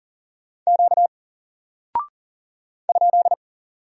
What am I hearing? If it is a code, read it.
Morse code, heard as a keyed tone, about 40 wpm, 700 Hz. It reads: Q ?
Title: Single Character (Letter:Number:Punctuation:Prosign) Alt 40wpm